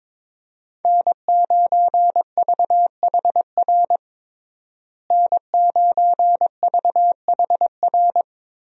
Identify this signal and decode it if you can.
Morse code, heard as a keyed tone, about 22 wpm, 700 Hz. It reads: N9VHR N9VHR
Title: USA Call Signs - 5 Characters - Repeated 3x - 22wpm